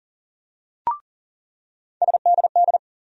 Morse code, heard as a keyed tone, about 40 wpm, 700 Hz. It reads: SDD
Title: USA Call Sign Postfixes - 3 Letters 40wpm